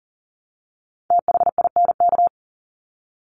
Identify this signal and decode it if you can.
Morse code, heard as a keyed tone, about 40 wpm, 700 Hz. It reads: THINK
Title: ING Suffix - Top 100 Words with Base Words 40wpm